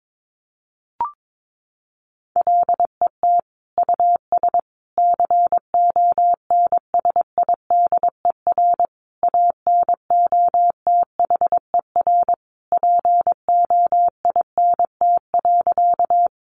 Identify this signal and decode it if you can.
Morse code, heard as a keyed tone, about 22 wpm, 700 Hz. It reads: LET US CONSIDER ANOTHER POINT.
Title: Sentences from Top 600 Words 22wpm